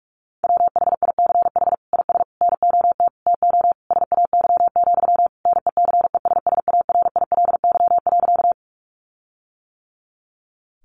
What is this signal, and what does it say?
Morse code, heard as a keyed tone, about 44 wpm, 700 Hz. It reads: WHICH IS NOT TO SAY, NECESSARILY.
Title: Sentences from Top 1400 Words 45wpm